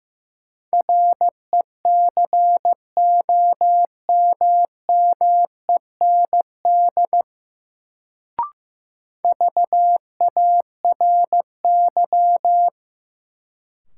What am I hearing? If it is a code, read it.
Morse code, heard as a keyed tone, about 15 wpm, 700 Hz. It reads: RECOMMEND VARY
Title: Sentences from Top 1200 Words - New Word Drill - 15wpm